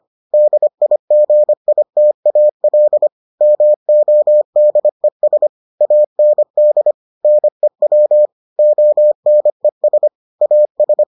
Morse code, heard as a keyed tone, about 25 wpm, 600 Hz. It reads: DIGITAL MODES AND NEW ONES AS